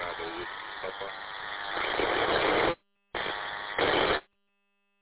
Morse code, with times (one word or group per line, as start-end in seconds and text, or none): none